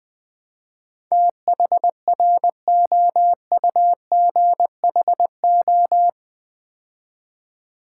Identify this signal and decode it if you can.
Morse code, heard as a keyed tone, about 20 wpm, 700 Hz. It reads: THROUGHO
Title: Build A Word - 4 Letters At A Time - 10-Letter Words - 20wpm